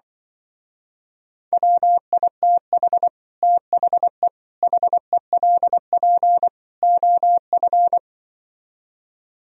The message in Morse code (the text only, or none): WITH THE HELP OF